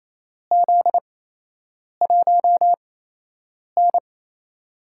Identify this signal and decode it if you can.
Morse code, heard as a keyed tone, about 28 wpm, 700 Hz. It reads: Z 1 N
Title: USA Call Signs - 4 Characters - 8x Character Spacing - 28wpm